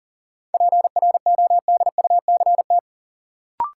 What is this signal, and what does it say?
Morse code, heard as a keyed tone, about 40 wpm, 700 Hz. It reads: PRODUCT
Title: Sentences from Top 1000 Words - Review Part 2 - New Word Drill - 40wpm